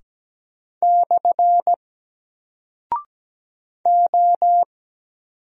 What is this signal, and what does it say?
Morse code, heard as a keyed tone, about 17 wpm, 700 Hz. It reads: / O
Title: Single Character (Letter:Number:Punctuation:Prosign) 17wpm